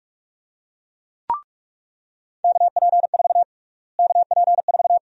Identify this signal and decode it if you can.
Morse code, heard as a keyed tone, about 47 wpm, 700 Hz. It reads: KP4 KP4
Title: USA Call Sign Prefixes - Repeated 2x - 45wpm